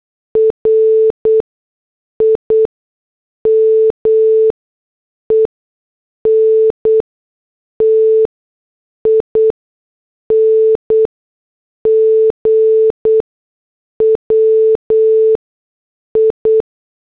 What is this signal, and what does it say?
Morse code, heard as a keyed tone, about 8 wpm, spaced as for 6 wpm, 440 Hz. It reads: RIMENTING WI
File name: morse / headlines